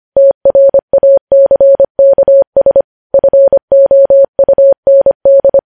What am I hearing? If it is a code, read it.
Morse code, heard as a keyed tone, about 25 wpm, 550 Hz. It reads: TRACKS FOUND
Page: News Headlines in Morse Code at 25 WPM